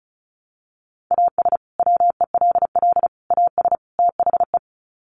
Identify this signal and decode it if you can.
Morse code, heard as a keyed tone, about 34 wpm, 700 Hz. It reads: AS WELL AS THE
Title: Sets of 4 Words 35wpm